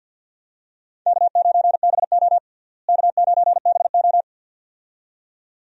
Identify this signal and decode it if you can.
Morse code, heard as a keyed tone, about 50 wpm, 700 Hz. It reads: K9BO K9BO
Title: USA Call Signs - 4 Characters - Repeated 2x - 50wpm